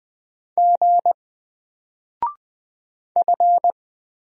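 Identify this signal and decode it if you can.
Morse code, heard as a keyed tone, about 20 wpm, 700 Hz. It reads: G F
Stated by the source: Single letter and number Morse drill at 20 wpm, from N4LCD.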